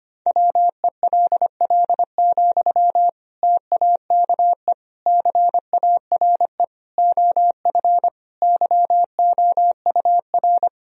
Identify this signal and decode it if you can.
Morse code, heard as a keyed tone, about 25 wpm, 700 Hz. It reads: WELL, TAKE CARE OF YOUR
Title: Sentences from Top 1000 Words - Review Part 1 - 25wpm